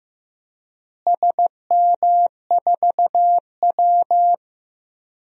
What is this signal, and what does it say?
Morse code, heard as a keyed tone, about 15 wpm, 700 Hz. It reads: SM4W
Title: DXCC-100 Call Signs - 4 Characters - Repeated 2x - 15wpm